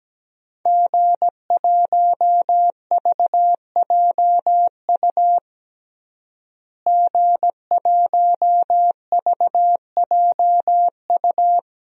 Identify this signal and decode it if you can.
Morse code, heard as a keyed tone, about 17 wpm, 700 Hz. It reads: G1VJU G1VJU
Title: DXCC-100 Call Signs - 5 Characters - Repeated 3x - 17wpm